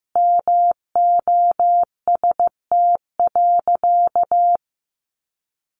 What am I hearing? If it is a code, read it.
Morse code, heard as a keyed tone, about 15 wpm, 700 Hz. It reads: MOST.
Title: Sentences from Top 1400 Words 15wpm